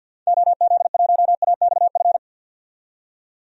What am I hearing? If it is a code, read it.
Morse code, heard as a keyed tone, about 52 wpm, 700 Hz. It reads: OZ1AXF
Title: DXCC-100 Call Signs - 6 Characters - 50wpm